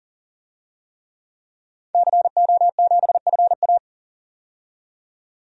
Morse code, heard as a keyed tone, about 40 wpm, 700 Hz. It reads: CO7FA
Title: DXCC-100 Call Signs - 5 Characters - 40wpm